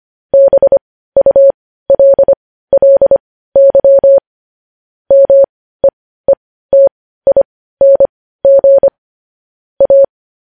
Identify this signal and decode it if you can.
Morse code, heard as a keyed tone, about 25 wpm, spaced as for 15 wpm, 550 Hz. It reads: BULLY MEETING A
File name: podcast / News headlines-1x-25x15